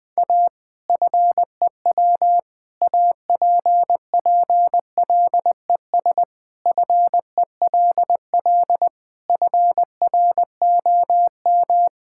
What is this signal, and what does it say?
Morse code, heard as a keyed tone, about 20 wpm, 700 Hz. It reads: A FEW APPLES FELL FROM
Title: Sentences with 2-Word Phrases - 20wpm